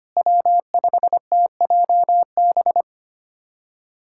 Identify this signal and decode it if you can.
Morse code, heard as a keyed tone, about 25 wpm, 700 Hz. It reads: W5TJB
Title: USA Callsign DE Callsign - 5 Characters - 25wpm